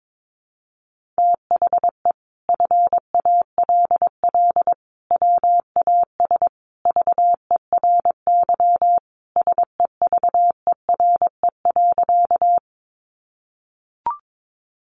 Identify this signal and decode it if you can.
Morse code, heard as a keyed tone, about 22 wpm, 700 Hz. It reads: THE FALL WAS VERY SEVERE.
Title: Sentences from Top 1400 Words 22wpm